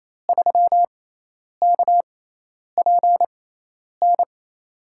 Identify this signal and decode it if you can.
Morse code, heard as a keyed tone, about 28 wpm, spaced as for 10 wpm, 700 Hz. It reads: 3KPN
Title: USA Call Signs - 5 Characters - 6x Character Spacing - 28wpm